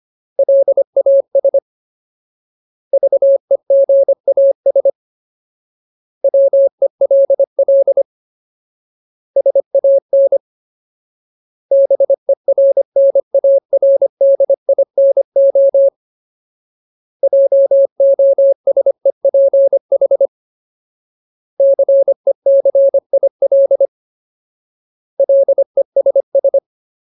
Morse code, heard as a keyed tone, about 25 wpm, 550 Hz. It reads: LAS VEGAS WELL SAN BERNARDINO JOSEPH CECIL LESS
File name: Ultimate Word List / 25WPM Ultimate